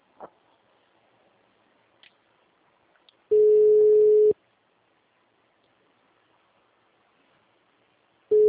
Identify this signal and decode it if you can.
Ringback tone (Europe, 425 Hz).
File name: out- -2038- - - .50082